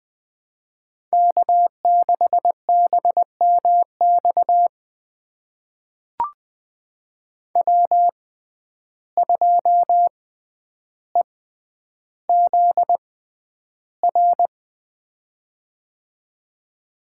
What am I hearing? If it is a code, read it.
Morse code, heard as a keyed tone, about 20 wpm, spaced as for 10 wpm, 700 Hz. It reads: K6BMX W2EZR
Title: USA Call Signs - 5 Characters - 6x Character Spacing - 20wpm